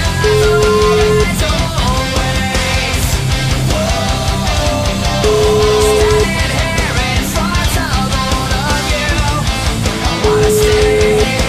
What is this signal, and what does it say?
Ringback tone (Europe, 425 Hz), with noise.